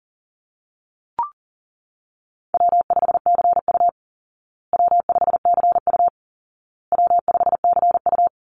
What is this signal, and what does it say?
Morse code, heard as a keyed tone, about 40 wpm, 700 Hz. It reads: W5CU W5CU W5CU
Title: Call Signs Repeated 2x - 40wpm